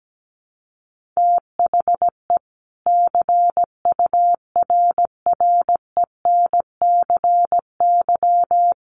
Morse code, heard as a keyed tone, about 17 wpm, 700 Hz. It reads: THE CURRENCY
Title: Sentences from Top 1500 Words 17wpm